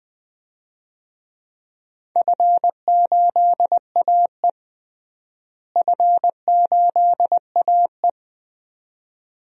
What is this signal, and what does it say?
Morse code, heard as a keyed tone, about 20 wpm, 700 Hz. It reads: F8AE F8AE
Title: DXCC-100 Call Signs - 4 Characters - Repeated 2x - 20wpm